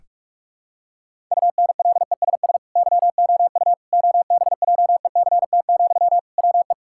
Morse code, heard as a keyed tone, about 42 wpm, 700 Hz. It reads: UNLESS YOU OBJECT, WE
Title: Sentences from Top 1000 Words - Review Part 6 - 45wpm